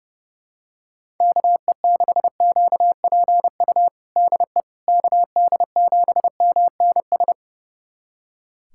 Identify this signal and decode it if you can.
Morse code, heard as a keyed tone, about 32 wpm, 700 Hz. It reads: KE6QPU DE KD7MNS
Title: USA Callsign DE Callsign 30wpm